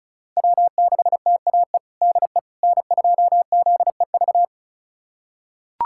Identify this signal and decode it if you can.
Morse code, heard as a keyed tone, about 36 wpm, 700 Hz. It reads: W6TAE DE N2ZEV
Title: USA Callsign DE Callsign - 5 Characters - 35wpm